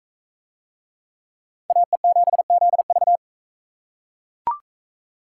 Morse code, heard as a keyed tone, about 39 wpm, 700 Hz. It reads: AE7ZV